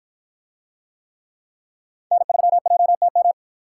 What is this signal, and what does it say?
Morse code, heard as a keyed tone, about 53 wpm, 700 Hz. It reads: N3JTM